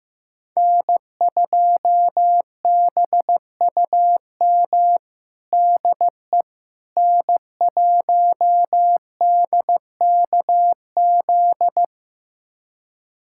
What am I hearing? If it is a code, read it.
Morse code, heard as a keyed tone, about 15 wpm, 700 Hz. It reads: N2BUM DE N1DKZ
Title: USA Callsign DE Callsign - 5 Characters - 15wpm